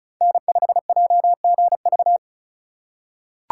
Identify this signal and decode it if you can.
Morse code, heard as a keyed tone, about 36 wpm, 690 Hz. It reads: N5JGV
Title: USA Call Signs - 5 Characters - 3x Character Spacing - 35wpm